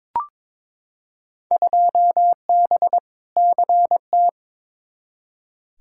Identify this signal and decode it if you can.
Morse code, heard as a keyed tone, about 22 wpm, 700 Hz. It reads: 2B CT
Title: Field Day Exchanges Encore - 22wpm